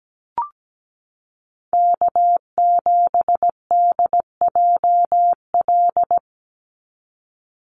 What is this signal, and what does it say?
Morse code, heard as a keyed tone, about 17 wpm, 700 Hz. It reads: K7DJL
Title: NA SOTA Chaser Callsigns 17wpm